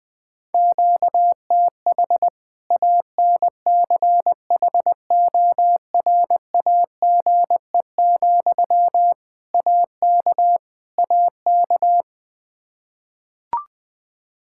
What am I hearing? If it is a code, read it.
Morse code, heard as a keyed tone, about 20 wpm, 700 Hz. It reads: QTH ANCHORAGE, AK AK
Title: Practice QSO - Exchange 1 - Beginning - 20wpm